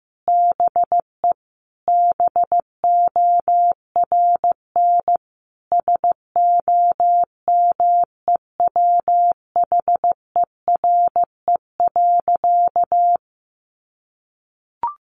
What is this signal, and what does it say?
Morse code, heard as a keyed tone, about 15 wpm, 700 Hz. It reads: BE BORN SOMEWHERE.
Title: Sentences from Top 1400 Words 15wpm